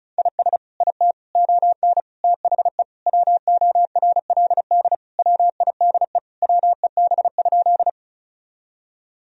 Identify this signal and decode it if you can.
Morse code, heard as a keyed tone, about 34 wpm, 700 Hz. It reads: IS IT ON THE WORLD WIDE WEB?
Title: Sentences from Top 1000 Words - Review Part 4 - 35wpm